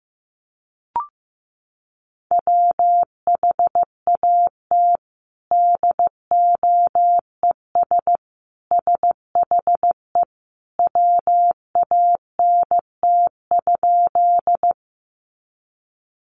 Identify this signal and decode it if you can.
Morse code, heard as a keyed tone, about 15 wpm, 700 Hz. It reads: WHAT DOES SHE WANT?
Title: Sentences from Top 200 Words 15wpm